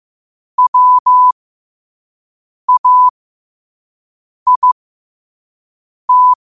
Morse code, heard as a keyed tone, about 15 wpm, spaced as for 5 wpm, 1000 Hz. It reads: WAIT
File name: qotdfiles / 5wpm